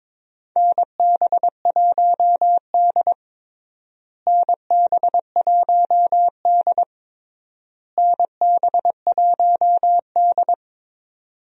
Morse code, heard as a keyed tone, about 22 wpm, 700 Hz. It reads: NB1D NB1D NB1D
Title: USA Call Signs - 4 Characters - Repeated 3x - 22wpm